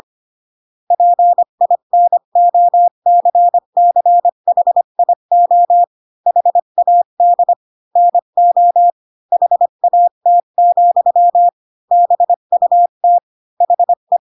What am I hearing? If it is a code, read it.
Morse code, heard as a keyed tone, about 25 wpm, 700 Hz. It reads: PINOCCHIO HAD NO HAT, BUT HE